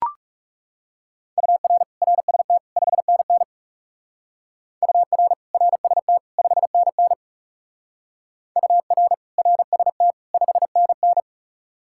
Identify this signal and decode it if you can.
Morse code, heard as a keyed tone, about 39 wpm, 700 Hz. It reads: UR RST 5NN UR RST 5NN UR RST 5NN